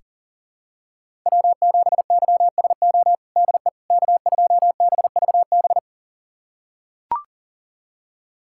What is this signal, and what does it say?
Morse code, heard as a keyed tone, about 40 wpm, 700 Hz. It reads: W7YSO DE K2BVB
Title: USA Callsign DE Callsign - 5 Characters - 40wpm